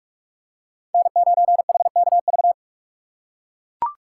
Morse code, heard as a keyed tone, about 42 wpm, 700 Hz. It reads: N9HKV